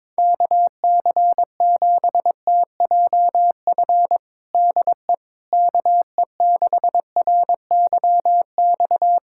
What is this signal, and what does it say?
Morse code, heard as a keyed tone, about 22 wpm, 700 Hz. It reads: KC7TJF DE KE6RYX
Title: USA Callsign DE Callsign - 6 Characters - 22wpm